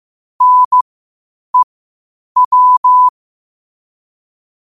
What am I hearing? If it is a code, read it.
Morse code, heard as a keyed tone, about 15 wpm, spaced as for 8 wpm, 1000 Hz. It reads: NEW
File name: qotdfiles / 8wpm